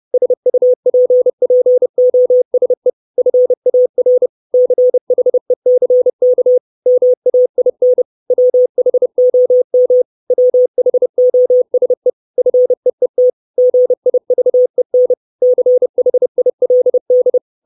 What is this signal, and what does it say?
Morse code, heard as a keyed tone, about 30 wpm, 500 Hz. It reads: SUPPOSE FAR CHECK MAIN WHOM WHOSE FEET GIVEN CHILD
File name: 632 Words 30WPM